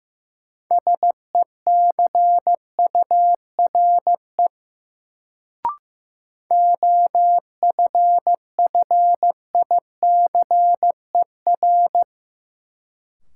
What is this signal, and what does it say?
Morse code, heard as a keyed tone, about 15 wpm, 700 Hz. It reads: SECURE OFFICER
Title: Sentences from Top 1200 Words - New Word Drill - 15wpm